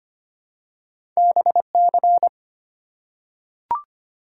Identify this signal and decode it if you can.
Morse code, heard as a keyed tone, about 25 wpm, 700 Hz. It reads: BC
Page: morse-code-ninja-practice-25wpm